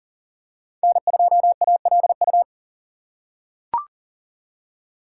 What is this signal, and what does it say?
Morse code, heard as a keyed tone, about 40 wpm, 700 Hz. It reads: N2ALU
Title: USA Call Signs - 2x Character Spacing - 40wpm